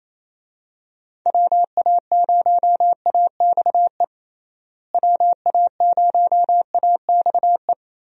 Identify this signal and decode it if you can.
Morse code, heard as a keyed tone, about 28 wpm, 700 Hz. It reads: WA0AXE WA0AXE
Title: USA Call Signs - 6 Characters - Repeated 2x - 28wpm